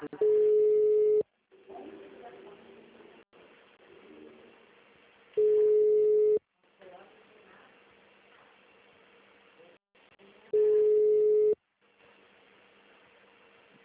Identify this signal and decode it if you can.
Ringback tone (Europe, 425 Hz), with speech.